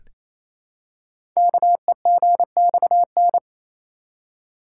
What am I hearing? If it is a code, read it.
Morse code, heard as a keyed tone, about 29 wpm, 700 Hz. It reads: KEGXN
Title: Code Groups - 5 Letters - 28wpm